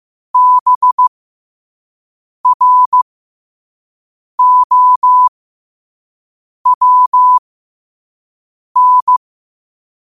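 Morse code, heard as a keyed tone, about 15 wpm, spaced as for 5 wpm, 1000 Hz. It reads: BROWN
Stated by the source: Morse code quote practice at 5 words per minute.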